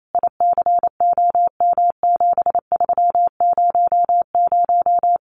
Morse code, heard as a keyed tone, about 28 wpm, 700 Hz. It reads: ICOM7300
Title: Common QSO Phrases 28wpm